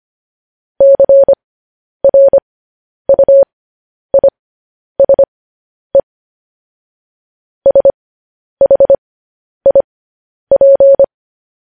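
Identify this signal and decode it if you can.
Morse code, heard as a keyed tone, about 25 wpm, spaced as for 10 wpm, 550 Hz. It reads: CRUISE SHIP
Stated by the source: Morse code news headlines practice, at 10 WPM.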